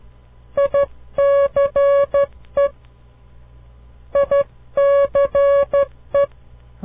Morse code, heard as a keyed tone, about 13 wpm, 560 Hz. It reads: ICE ICE
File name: CW Learning Programs / farnsworth / unit 2